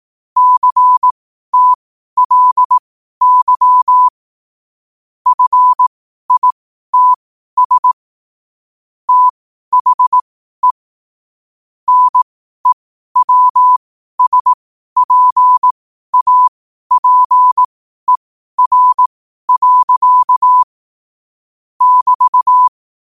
Morse code, heard as a keyed tone, about 18 wpm, spaced as for 12 wpm, 1000 Hz. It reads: CTLY FITS THE NEWSPAPER. <BT>
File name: qotdfiles / 12wpm